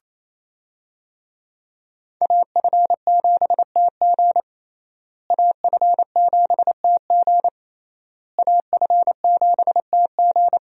Morse code, heard as a keyed tone, about 28 wpm, 700 Hz. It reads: AF7TG AF7TG AF7TG
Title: USA Call Signs - 5 Characters - Repeated 3x - 28wpm